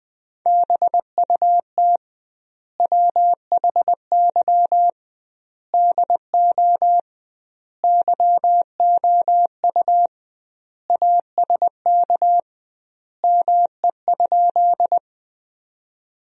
Morse code, heard as a keyed tone, about 20 wpm, 700 Hz. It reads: BUT WHY DO YOU ASK ME?
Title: 2x Word-Spacing - Sentences from Top 200 Words - 20wpm